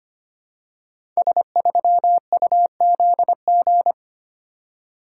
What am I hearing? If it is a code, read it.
Morse code, heard as a keyed tone, about 25 wpm, 700 Hz. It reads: S3UZG
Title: Code Groups - 5 Letter:Numbers - 25wpm